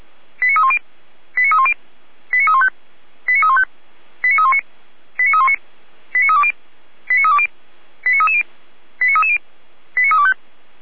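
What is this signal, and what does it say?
Five-tone selective call (ZVEI), heard as quick runs of five tones, tones of about 70 ms. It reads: 89310 89310 89315 89315 89319 89319 89320 89320 89330 89330 89325
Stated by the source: ZVEI five-tone call received on SDR, recorded by Feuerwehr Orlamünde.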